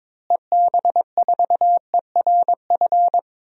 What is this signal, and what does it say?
Morse code, heard as a keyed tone, about 21 wpm, 700 Hz. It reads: EB4ERF
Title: DXCC-100 Call Signs - 6 Characters - Repeated 2x - 22wpm